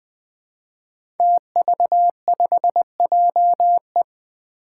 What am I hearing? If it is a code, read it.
Morse code, heard as a keyed tone, about 20 wpm, 700 Hz. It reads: TV5JE
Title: Five Challenging Letter-Number Combinations 20wpm